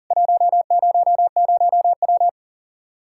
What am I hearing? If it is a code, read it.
Morse code, heard as a keyed tone, about 40 wpm, 700 Hz. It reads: 100W
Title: K1USN SST QSO Practice 40wpm